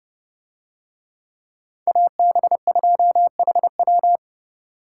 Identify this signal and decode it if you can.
Morse code, heard as a keyed tone, about 28 wpm, 700 Hz. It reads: AB2HW